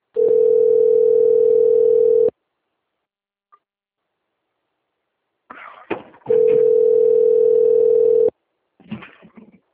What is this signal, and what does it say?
Ringback tone (North America), 440 + 480 Hz.
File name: rg-1600-unknown- - - .27067